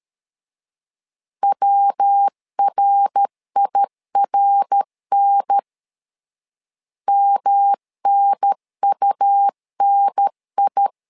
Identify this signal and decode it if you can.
Morse code, heard as a keyed tone, about 13 wpm, 800 Hz. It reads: WRIRN MNUNI
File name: LEZIONI / 3 MWRO / 3 MWR 12wpm 1711